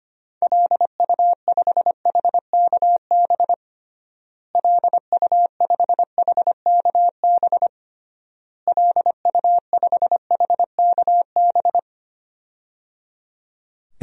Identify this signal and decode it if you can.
Morse code, heard as a keyed tone, about 25 wpm, 700 Hz. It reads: LU5HKB LU5HKB LU5HKB
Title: DXCC-100 Call Signs - 6 Characters - Repeated 3x - 25wpm